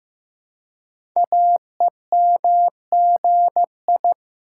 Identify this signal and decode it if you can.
Morse code, heard as a keyed tone, about 15 wpm, 700 Hz. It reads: AEMGI
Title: Code Groups - 5 Letters - 15wpm